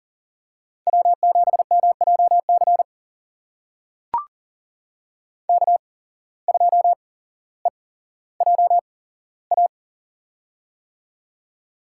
Morse code, heard as a keyed tone, about 42 wpm, spaced as for 17 wpm, 700 Hz. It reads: W7MJC K2EJA
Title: USA Call Signs - 5 Characters - 8x Character Spacing - 40wpm